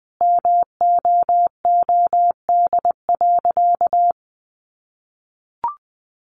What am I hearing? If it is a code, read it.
Morse code, heard as a keyed tone, about 20 wpm, 700 Hz. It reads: MOOD.
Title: Sentences from Top 1100 Words 20wpm